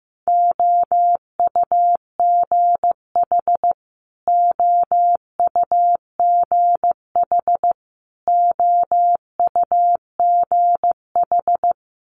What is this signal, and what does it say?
Morse code, heard as a keyed tone, about 15 wpm, 700 Hz. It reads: OUGH OUGH OUGH
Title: Multi-letter Phonemes Repeated 3x 15wpm